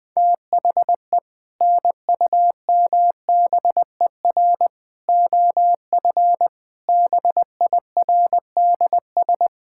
Morse code, heard as a keyed tone, about 20 wpm, 700 Hz. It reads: THE NUMBER OF BIRDS
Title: Sentences with 3-Word Phrases - 20wpm